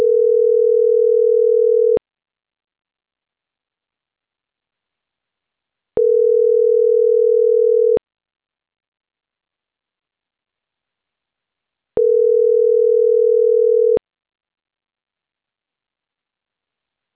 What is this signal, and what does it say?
Ringback tone (North America), 440 + 480 Hz.